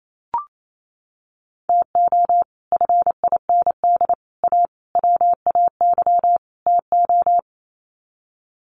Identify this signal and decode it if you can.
Morse code, heard as a keyed tone, about 28 wpm, 700 Hz. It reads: TO FIND A WAY TO
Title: Sets of 5 Words 28wpm